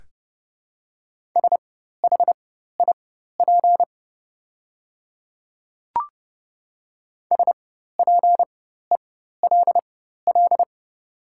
Morse code, heard as a keyed tone, about 30 wpm, spaced as for 14 wpm, 690 Hz. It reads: SHIP SPELL
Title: 4x Character Spacing - Top 500 Words - 30wpm